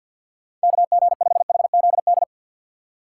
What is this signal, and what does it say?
Morse code, heard as a keyed tone, about 50 wpm, 700 Hz. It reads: KG5HZD
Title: USA Call Signs - 6 Characters - 4x Character Spacing - 50wpm